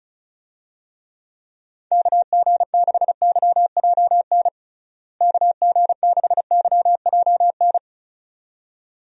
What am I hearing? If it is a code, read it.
Morse code, heard as a keyed tone, about 36 wpm, 700 Hz. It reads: KG6YJN KG6YJN
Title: USA Call Signs - 6 Characters - Repeated 2x - 35wpm